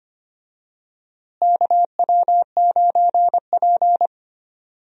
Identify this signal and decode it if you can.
Morse code, heard as a keyed tone, about 25 wpm, 700 Hz. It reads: KW9P
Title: USA Call Signs - 4 Characters - Repeated 3x - 25wpm